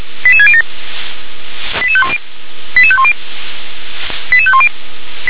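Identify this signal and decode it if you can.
Five-tone selective call (ZVEI), heard as quick runs of five tones, tones of about 70 ms. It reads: 88607 88510 88510 88510
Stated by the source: Five-tone ZVEI selective call received on SDR.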